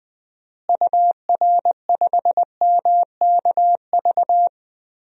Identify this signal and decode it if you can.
Morse code, heard as a keyed tone, about 20 wpm, 700 Hz. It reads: UR5MKV
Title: DXCC-100 Call Signs - 6 Characters - Repeated 2x - 20wpm